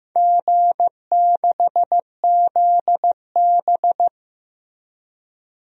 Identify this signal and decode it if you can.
Morse code, heard as a keyed tone, about 15 wpm, 700 Hz. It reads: G6ZB
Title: DXCC-100 Call Signs - 4 Characters - Repeated 3x - 15wpm